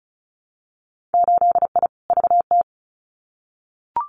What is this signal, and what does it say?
Morse code, heard as a keyed tone, about 36 wpm, 700 Hz. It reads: 8I VT
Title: Winter Field Day Exchanges Encore 35wpm